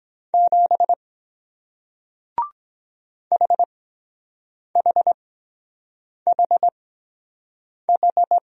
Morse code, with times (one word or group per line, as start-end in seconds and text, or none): 0.3-0.9 7
3.3-8.4 HHHH